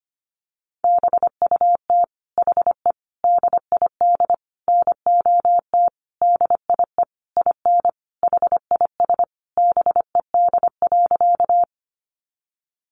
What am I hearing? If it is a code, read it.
Morse code, heard as a keyed tone, about 25 wpm, 700 Hz. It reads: BUT HE DID NOT DIE IN HIS BED.
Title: Sentences from Top 1300 Words 25wpm